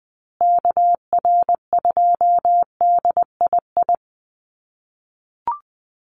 Morse code, heard as a keyed tone, about 20 wpm, 700 Hz. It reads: KR2DII
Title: USA Call Signs Repeated 2x 20wpm